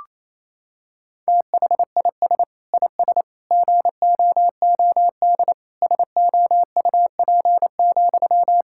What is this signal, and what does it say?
Morse code, heard as a keyed tone, about 28 wpm, 700 Hz. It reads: THIS IS GOOD SOUP,
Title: Sentences from Top 1000 Words - Review Part 6 - 28wpm